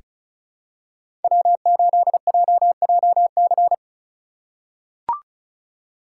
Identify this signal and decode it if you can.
Morse code, heard as a keyed tone, about 34 wpm, 690 Hz. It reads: W8JJC
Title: USA Call Signs - 5 Characters - 2x Character Spacing - 35wpm